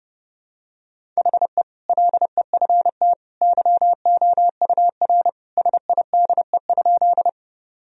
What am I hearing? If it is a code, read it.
Morse code, heard as a keyed tone, about 32 wpm, 700 Hz. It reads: HE LEFT YOUR SIDE?
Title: Sentences from Top 1000 Words - Review Part 1 - 30wpm